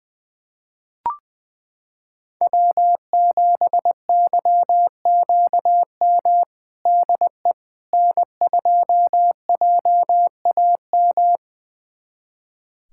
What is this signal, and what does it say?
Morse code, heard as a keyed tone, about 20 wpm, 700 Hz. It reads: W7YQM DE N2JAM
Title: USA Callsign DE Callsign - 5 Characters - 20wpm